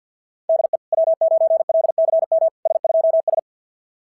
Morse code, heard as a keyed tone, about 48 wpm, 660 Hz. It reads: DE W9LCM S2S
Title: SOTA QSO Practice 50wpm